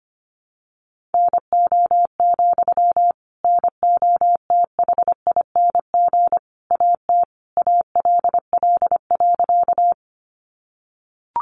Morse code, heard as a keyed tone, about 25 wpm, 700 Hz. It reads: NO, NOTHING AT ALL.
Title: Sentences from Top 500 Words 25wpm